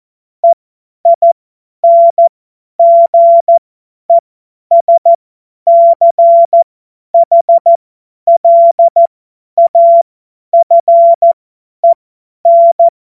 Morse code, heard as a keyed tone, about 14 wpm, spaced as for 10 wpm, 680 Hz. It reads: EINGESCHLAFEN
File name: Morsekurs / Witze / Witz 26. Oktober 2021, 50 F70